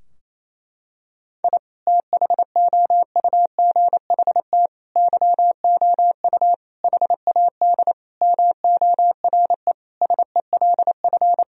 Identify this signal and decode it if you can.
Morse code, heard as a keyed tone, about 28 wpm, 700 Hz. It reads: I THOUGHT YOU HAD MORE SELF
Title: Sentences from Top 1000 Words - Review Part 3 - 28wpm